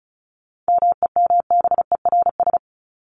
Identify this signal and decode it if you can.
Morse code, heard as a keyed tone, about 34 wpm, 700 Hz. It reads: MEMBERS
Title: ES:S Suffix - Top 100 Words with Base Words 35wpm